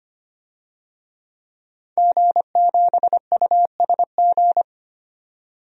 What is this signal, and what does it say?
Morse code, heard as a keyed tone, about 25 wpm, 700 Hz. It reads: G7USG
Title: DXCC-100 Call Signs - 5 Characters - 25wpm